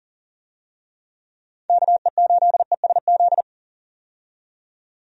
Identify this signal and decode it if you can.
Morse code, heard as a keyed tone, about 40 wpm, 700 Hz. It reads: KE8ESZ
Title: USA Call Signs - 6 Characters - 40wpm